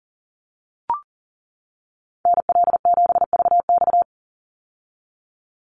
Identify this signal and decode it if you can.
Morse code, heard as a keyed tone, about 40 wpm, 700 Hz. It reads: NL7VX
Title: Call Signs - 40wpm